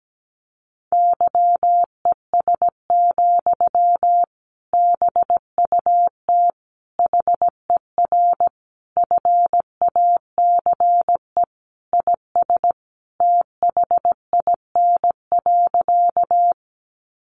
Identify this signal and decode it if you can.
Morse code, heard as a keyed tone, about 17 wpm, 700 Hz. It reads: YES, BUT HER FACE IS THIN.
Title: Sentences from Top 1200 Words 17wpm